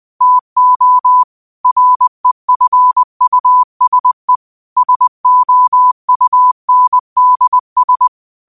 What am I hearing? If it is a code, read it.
Morse code, heard as a keyed tone, about 20 wpm, 1000 Hz. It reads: TO REFUSE SOUNDS
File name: qotdfiles / 20wpm